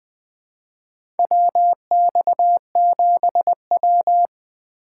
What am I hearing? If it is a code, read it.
Morse code, heard as a keyed tone, about 20 wpm, 700 Hz. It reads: WX7W
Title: USA Call Signs - 4 Characters - 4x Character Spacing - 20wpm